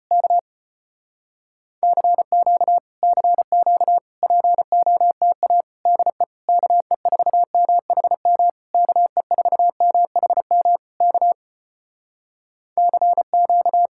Morse code, heard as a keyed tone, about 34 wpm, 700 Hz. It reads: K CQ CQ POTA DE KE4MHM KE4MHM K CQ